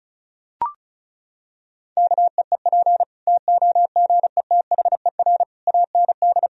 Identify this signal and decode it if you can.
Morse code, heard as a keyed tone, about 36 wpm, 700 Hz. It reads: KEEP TOGETHER AND
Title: Sentences from Top 1000 Words - Review Part 1 - 35wpm